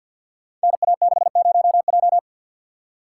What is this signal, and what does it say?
Morse code, heard as a keyed tone, about 51 wpm, 700 Hz. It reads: NA60J
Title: Code Groups - 5 Letter:Numbers - 50wpm